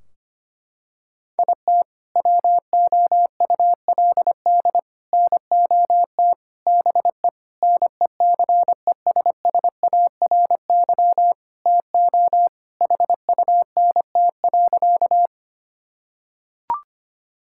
Morse code, heard as a keyed tone, about 25 wpm, 700 Hz. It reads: IT WOULD NOT BE NECESSARY TO HUNT.
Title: Sentences from Top 1600 Words 25wpm